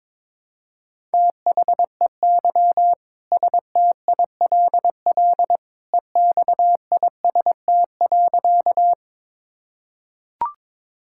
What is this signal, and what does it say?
Morse code, heard as a keyed tone, about 22 wpm, 700 Hz. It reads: THEY STILL EXIST.
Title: Sentences from Top 1000 Words - Review Part 5 - 22wpm